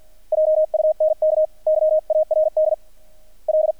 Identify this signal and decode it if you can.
Morse code, heard as a keyed tone, about 48 wpm, spaced as for 41 wpm, 620 Hz. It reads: JUNKYARD L
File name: archived / CRSnet